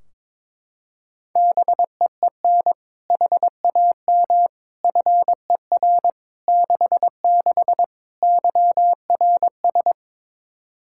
Morse code, heard as a keyed tone, about 22 wpm, 700 Hz. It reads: BEEN HAM FER 66 YRS
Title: Practice QSO - Exchange 3 - More Info - 22wpm